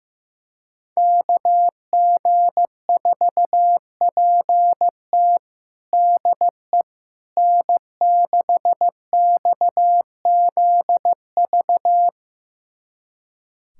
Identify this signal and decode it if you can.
Morse code, heard as a keyed tone, about 16 wpm, 700 Hz. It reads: KG4PT DE N6XZV
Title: USA Callsign DE Callsign - 5 Characters - 15wpm